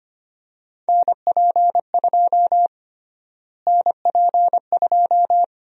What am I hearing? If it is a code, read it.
Morse code, heard as a keyed tone, about 25 wpm, 700 Hz. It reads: NP2 NP2
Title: USA Call Sign Prefixes - Repeated 2x - 25wpm